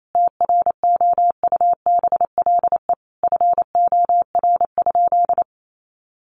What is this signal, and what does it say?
Morse code, heard as a keyed tone, about 28 wpm, 700 Hz. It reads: TROUBLE FOR?
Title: Sentences from Top 1200 Words 28wpm